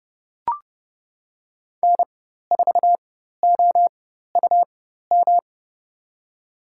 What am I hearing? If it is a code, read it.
Morse code, heard as a keyed tone, about 30 wpm, spaced as for 14 wpm, 700 Hz. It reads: N4OUM
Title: USA Call Signs - 5 Characters - 4x Character Spacing - 30wpm